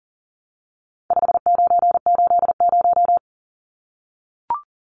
Morse code, heard as a keyed tone, about 40 wpm, 700 Hz. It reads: 5980
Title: Numbers - 4-Digit 40wpm